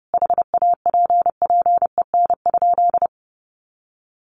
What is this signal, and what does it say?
Morse code, heard as a keyed tone, about 30 wpm, 700 Hz. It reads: HAPPEN?
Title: Sentences from Top 400 Words 30wpm